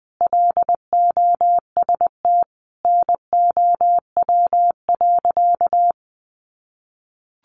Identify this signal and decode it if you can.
Morse code, heard as a keyed tone, about 20 wpm, 700 Hz. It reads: LOST NOW.
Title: Sentences from Top 1400 Words 20wpm